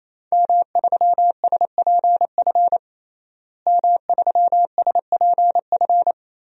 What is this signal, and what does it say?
Morse code, heard as a keyed tone, about 27 wpm, 700 Hz. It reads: M3SPF M3SPF
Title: DXCC-100 Call Signs - 5 Characters - Repeated 3x - 28wpm